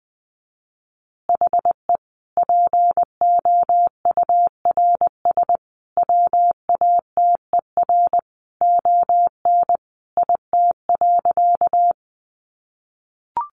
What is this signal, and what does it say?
Morse code, heard as a keyed tone, about 20 wpm, 700 Hz. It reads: HE POURS WATER ON IT.
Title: Additional Easy Phrases 20wpm